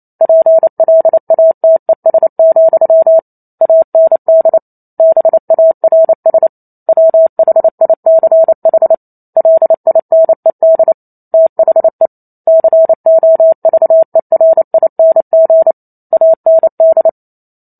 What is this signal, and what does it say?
Morse code, heard as a keyed tone, about 28 wpm, 650 Hz. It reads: PLATES, AND BARS WHICH LINED THE COVERING AND